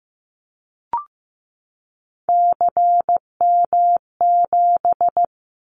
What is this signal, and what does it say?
Morse code, heard as a keyed tone, about 15 wpm, 700 Hz. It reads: CM7
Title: Callsign Prefixes for Top 100 DXCC - 15wpm